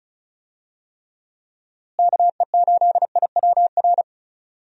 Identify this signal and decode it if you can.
Morse code, heard as a keyed tone, about 36 wpm, 700 Hz. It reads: KE8IWR
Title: USA Call Signs - 6 Characters - Repeated 2x - 35wpm